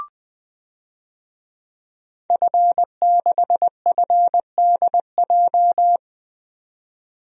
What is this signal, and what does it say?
Morse code, heard as a keyed tone, about 20 wpm, 700 Hz. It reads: F6FDJ
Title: DXCC-100 Call Signs - 5 Characters - 20wpm